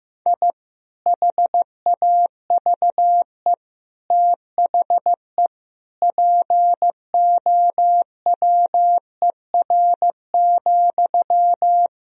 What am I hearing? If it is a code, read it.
Morse code, heard as a keyed tone, about 15 wpm, 700 Hz. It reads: I HAVE THE POWER,
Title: Sentences from Top 1000 Words - Review Part 2 - 15wpm